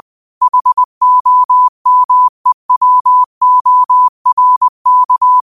Morse code, heard as a keyed tone, about 20 wpm, 1000 Hz. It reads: HOMEWORK